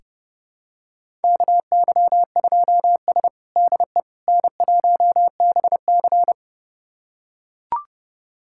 Morse code, heard as a keyed tone, about 30 wpm, 700 Hz. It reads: KY2S DE N1BC
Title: USA Callsign DE Callsign - 4 Characters - 30wpm